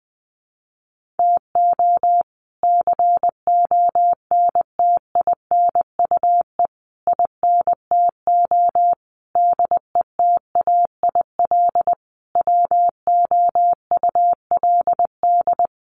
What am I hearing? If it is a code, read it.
Morse code, heard as a keyed tone, about 20 wpm, 700 Hz. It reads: TO CONTINUE INTO DETAIL WOULD
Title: Sentences from Top 1500 Words 20wpm